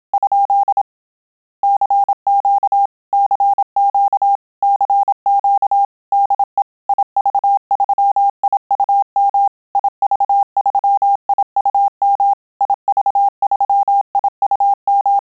Morse code, heard as a keyed tone, about 27 wpm, 800 Hz. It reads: ? CQ CQ CQ DE IV3IUM IV3IUM IV3IUM
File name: Telegraphy-group / QSO CW 25 PPM